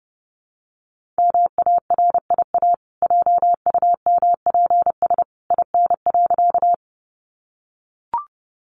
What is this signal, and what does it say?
Morse code, heard as a keyed tone, about 30 wpm, 700 Hz. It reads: MARIA JUMPS IN.
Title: Easy Phrases 30wpm